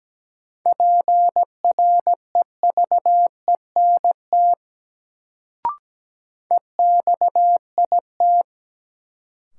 Morse code, heard as a keyed tone, about 17 wpm, 700 Hz. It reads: PREVENT EXIT
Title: Sentences from Top 1100 Words - New Word Drill Alt - 17wpm